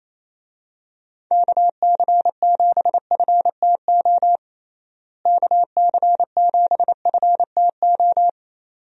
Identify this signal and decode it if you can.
Morse code, heard as a keyed tone, about 28 wpm, 700 Hz. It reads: KC7FTO KC7FTO
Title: USA Call Signs - 6 Characters - Repeated 2x - 28wpm